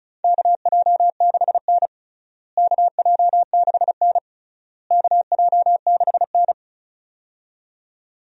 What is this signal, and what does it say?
Morse code, heard as a keyed tone, about 34 wpm, 700 Hz. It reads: KJ6N KJ6N KJ6N
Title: USA Call Signs - 4 Characters - Repeated 3x - 35wpm